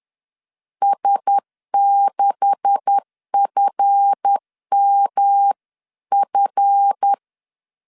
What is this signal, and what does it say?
Morse code, heard as a keyed tone, about 10 wpm, 800 Hz. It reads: S6FMF